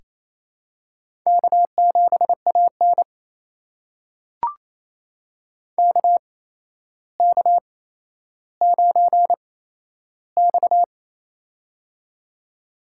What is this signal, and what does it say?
Morse code, heard as a keyed tone, about 28 wpm, spaced as for 12 wpm, 700 Hz. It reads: K7AN KK9X
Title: USA Call Signs - 4 Characters - 8x Character Spacing - 28wpm